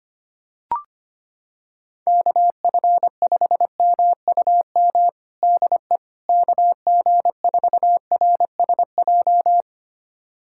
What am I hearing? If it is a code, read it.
Morse code, heard as a keyed tone, about 25 wpm, 700 Hz. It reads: KF5MUM DE KG4RSJ
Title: USA Callsign DE Callsign - 6 Characters - 25wpm